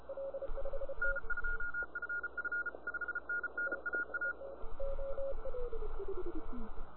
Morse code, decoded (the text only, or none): N2FF 5NN AA